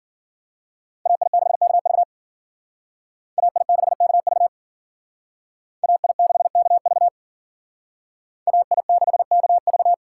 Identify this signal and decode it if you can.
Morse code, heard as a keyed tone, about 51 wpm, 700 Hz. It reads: AI6KV AI6KV AI6KV AI6KV